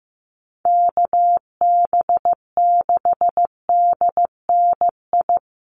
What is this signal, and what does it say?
Morse code, heard as a keyed tone, about 15 wpm, 700 Hz. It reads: KB6DNI
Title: USA Call Signs 15wpm